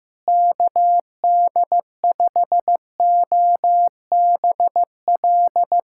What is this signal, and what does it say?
Morse code, heard as a keyed tone, about 15 wpm, 700 Hz. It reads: KD5OBL
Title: USA Callsign DE Callsign 15wpm